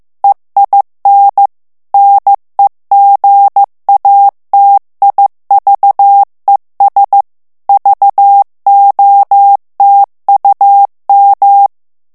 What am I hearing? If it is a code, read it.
Morse code, heard as a keyed tone, about 15 wpm, 800 Hz. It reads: EIN NEGATIVES VOTUM
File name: morse mp3 / ALLEMAND texte clair 1 vitesse 14 mots minute